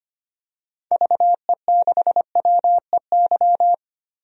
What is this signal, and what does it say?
Morse code, heard as a keyed tone, about 25 wpm, 700 Hz. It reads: VE6WEY
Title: DXCC-100 Call Signs - 6 Characters - Repeated 2x - 25wpm